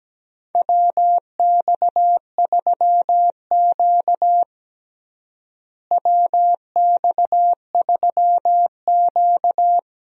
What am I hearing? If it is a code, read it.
Morse code, heard as a keyed tone, about 17 wpm, 700 Hz. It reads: WX3Q WX3Q
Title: USA Call Signs - 4 Characters - Repeated 2x - 17wpm